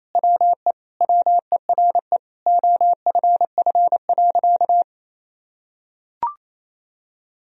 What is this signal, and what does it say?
Morse code, heard as a keyed tone, about 28 wpm, 700 Hz. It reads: WE WERE OFF.
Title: Sentences from Top 1000 Words - Review Part 1 - 28wpm